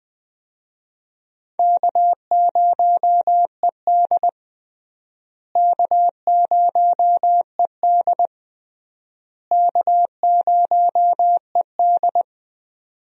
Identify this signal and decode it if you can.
Morse code, heard as a keyed tone, about 20 wpm, 700 Hz. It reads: K0ED K0ED K0ED
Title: USA Call Signs - 4 Characters - Repeated 3x - 20wpm